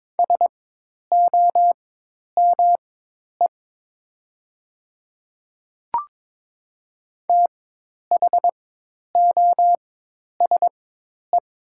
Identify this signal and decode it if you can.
Morse code, heard as a keyed tone, about 22 wpm, spaced as for 10 wpm, 700 Hz. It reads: SOME THOSE
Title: 4x Character Spacing - Top 300 Words - 22wpm